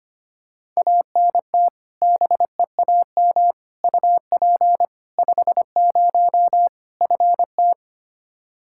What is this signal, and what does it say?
Morse code, heard as a keyed tone, about 25 wpm, 700 Hz. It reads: ANT BEAM UP 50 FT
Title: K1USN SST QSO Practice 25wpm